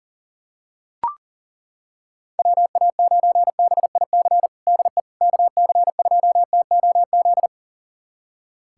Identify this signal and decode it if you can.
Morse code, heard as a keyed tone, about 40 wpm, 700 Hz. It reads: WA9BIC DE KC2TOZ
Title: USA Callsign DE Callsign 40wpm